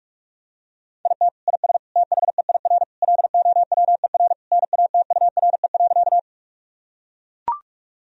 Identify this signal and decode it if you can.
Morse code, heard as a keyed tone, about 42 wpm, 700 Hz. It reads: IT IS THEIR LOWER NATURE.
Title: Sentences from Top 1000 Words - Review Part 4 - 45wpm